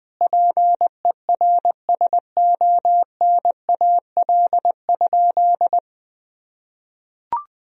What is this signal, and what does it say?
Morse code, heard as a keyed tone, about 20 wpm, 700 Hz. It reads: PERSONAL?
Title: Sentences from Top 1000 Words - Review Part 4 - 20wpm